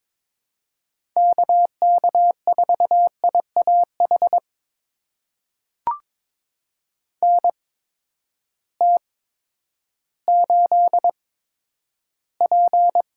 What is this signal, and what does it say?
Morse code, heard as a keyed tone, about 22 wpm, spaced as for 11 wpm, 700 Hz. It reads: KK4IAH NT8P
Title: USA Call Signs - 8x Character Spacing - 22wpm